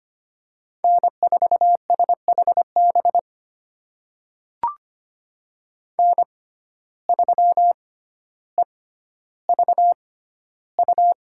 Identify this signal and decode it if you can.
Morse code, heard as a keyed tone, about 25 wpm, spaced as for 13 wpm, 700 Hz. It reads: N4SHB N3EVU
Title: USA Call Signs - 5 Characters - 6x Character Spacing - 25wpm